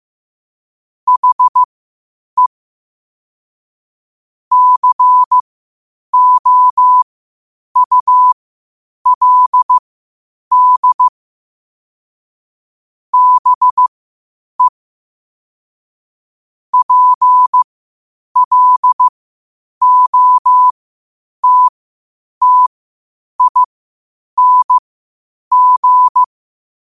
Morse code, heard as a keyed tone, about 15 wpm, spaced as for 8 wpm, 1000 Hz. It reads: HE COULD BE PLOTTING